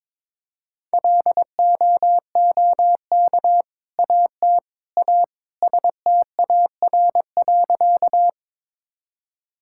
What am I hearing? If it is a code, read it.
Morse code, heard as a keyed tone, about 22 wpm, 700 Hz. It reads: LOOK AT A STAR.
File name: practice / 22wpm / Sentences with 2-Word Phrases - 22wpm